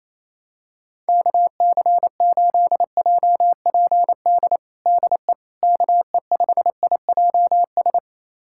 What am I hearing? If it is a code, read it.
Morse code, heard as a keyed tone, about 28 wpm, 700 Hz. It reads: KC8JPD DE KE5IJS
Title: USA Callsign DE Callsign - 6 Characters - 28wpm